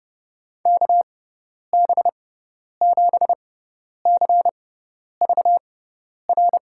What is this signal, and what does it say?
Morse code, heard as a keyed tone, about 32 wpm, spaced as for 10 wpm, 700 Hz. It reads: KB7CVR
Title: USA Call Signs - 6 Characters - 6x Character Spacing - 30wpm